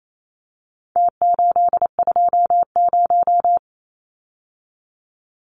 Morse code, heard as a keyed tone, about 28 wpm, 700 Hz. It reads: T820
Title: Numbers - Contest Style - 4-Digit 28wpm